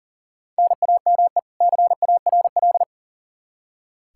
Morse code, heard as a keyed tone, about 40 wpm, 700 Hz. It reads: NAME CARL
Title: Practice QSO - Exchange 1 - Beginning - 40wpm